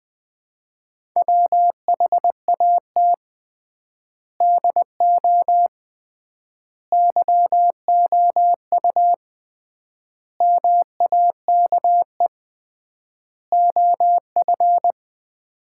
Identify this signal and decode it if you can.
Morse code, heard as a keyed tone, about 21 wpm, 700 Hz. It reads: WHAT DO YOU MAKE OF
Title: 3x Word-Spacing - Sets of 5 Words - 20wpm